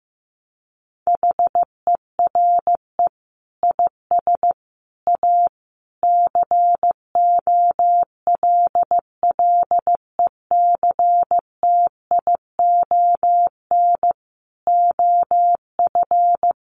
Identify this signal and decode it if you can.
Morse code, heard as a keyed tone, about 15 wpm, 700 Hz. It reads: HERE IS A COLLECTION OF
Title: Sentences from Top 1200 Words 15wpm